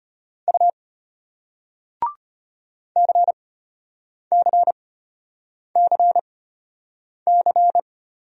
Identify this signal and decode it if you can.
Morse code, heard as a keyed tone, about 34 wpm, spaced as for 8 wpm, 700 Hz. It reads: U CCCC